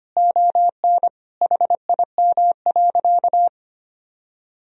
Morse code, heard as a keyed tone, about 25 wpm, 700 Hz. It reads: ON HIM.
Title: Sentences from Top 1000 Words - Review Part 1 - 25wpm